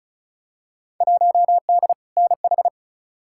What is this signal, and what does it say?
Morse code, heard as a keyed tone, about 34 wpm, 700 Hz. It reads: 1D NH
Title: Field Day Exchanges - 35wpm